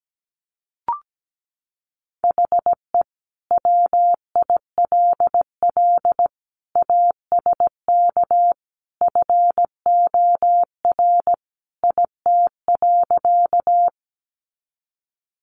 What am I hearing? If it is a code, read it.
Morse code, heard as a keyed tone, about 17 wpm, 700 Hz. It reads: HE WILL ASK FOR IT.
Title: Sentences from Top 200 Words 17wpm